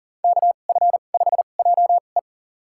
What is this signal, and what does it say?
Morse code, heard as a keyed tone, about 40 wpm, spaced as for 29 wpm, 700 Hz. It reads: KF5JE
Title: USA Call Signs - 2x Character Spacing - 40wpm